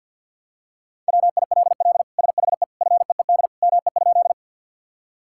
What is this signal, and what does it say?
Morse code, heard as a keyed tone, about 50 wpm, 700 Hz. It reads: WILL SHE FEED ME?
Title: Sentences from Top 1600 Words 50wpm